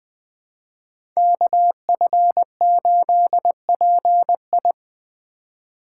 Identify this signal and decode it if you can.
Morse code, heard as a keyed tone, about 20 wpm, 700 Hz. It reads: KF8PI
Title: USA Call Signs - 5 Characters - 3x Character Spacing - 20wpm